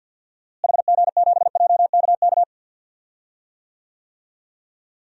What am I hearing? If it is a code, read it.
Morse code, heard as a keyed tone, about 50 wpm, 700 Hz. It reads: HG7JKK
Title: DXCC-100 Call Signs - 6 Characters - Repeated 3x - 50wpm